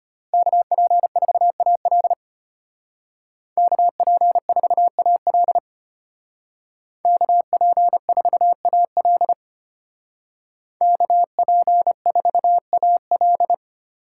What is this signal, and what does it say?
Morse code, heard as a keyed tone, about 30 wpm, 700 Hz. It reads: KP4AL KP4AL KP4AL KP4AL
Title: Speed Racing - USA Call Signs 25wpm